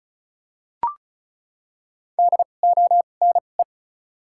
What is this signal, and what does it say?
Morse code, heard as a keyed tone, about 36 wpm, spaced as for 25 wpm, 700 Hz. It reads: DONE
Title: 2x Character Spacing - Top 500 Words - 35wpm